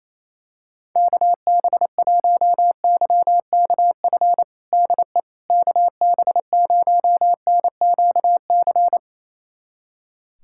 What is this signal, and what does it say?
Morse code, heard as a keyed tone, about 27 wpm, 700 Hz. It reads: KB1YKF DE KB0NQC
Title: USA Callsign DE Callsign - 6 Characters - 28wpm